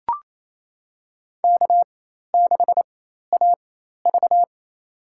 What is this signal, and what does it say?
Morse code, heard as a keyed tone, about 28 wpm, spaced as for 13 wpm, 700 Hz. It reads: K6AV
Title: USA Call Signs - 4 Characters - 4x Character Spacing - 28wpm